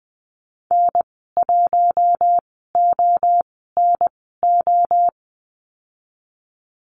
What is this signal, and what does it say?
Morse code, heard as a keyed tone, about 20 wpm, spaced as for 14 wpm, 700 Hz. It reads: N1ONO
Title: 055 Lesson 10g - Letter r - Call Signs 20wpm